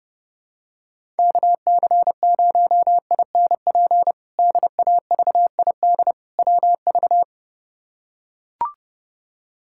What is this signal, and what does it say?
Morse code, heard as a keyed tone, about 30 wpm, 700 Hz. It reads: KC0INP DAVID WV
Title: K1USN SST QSO Practice - 30wpm